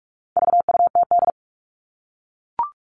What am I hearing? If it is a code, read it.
Morse code, heard as a keyed tone, about 44 wpm, 700 Hz. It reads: VUTD
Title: All Cut Numbers - 4-Digit 45wpm